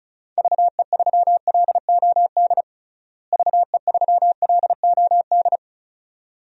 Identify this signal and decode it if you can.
Morse code, heard as a keyed tone, about 34 wpm, 700 Hz. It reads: VE3LOD VE3LOD
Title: DXCC-100 Call Signs - 6 Characters - Repeated 2x - 35wpm